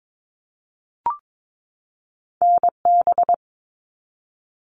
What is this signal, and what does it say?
Morse code, heard as a keyed tone, about 22 wpm, 700 Hz. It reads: NB
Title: Canadian Provinces and Territories 22wpm